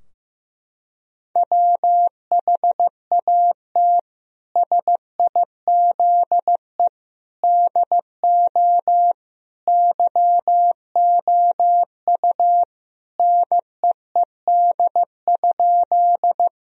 Morse code, heard as a keyed tone, about 15 wpm, 700 Hz. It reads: WHAT SIZE DO YOU NEED?
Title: Sentences from Top 1000 Words - Review Part 2 - 15wpm